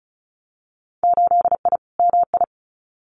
Morse code, heard as a keyed tone, about 36 wpm, 700 Hz. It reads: 8I MI
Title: Winter Field Day Exchanges Encore 35wpm